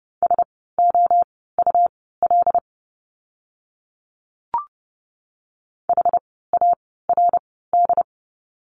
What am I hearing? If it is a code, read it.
Morse code, heard as a keyed tone, about 30 wpm, spaced as for 17 wpm, 700 Hz. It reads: SOUL HARD
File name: Lesson 17f - Letter u - Words 30wpm